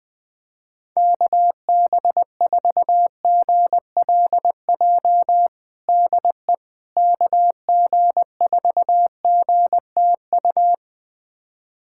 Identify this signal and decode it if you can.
Morse code, heard as a keyed tone, about 21 wpm, 700 Hz. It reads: KB4GLJ DE KG4GTU
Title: USA Callsign DE Callsign - 6 Characters - 20wpm